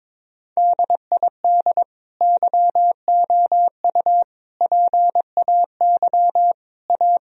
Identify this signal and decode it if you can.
Morse code, heard as a keyed tone, about 22 wpm, 700 Hz. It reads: DID YOU PAY A
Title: Sentences from Top 1000 Words - Review Part 4 - 22wpm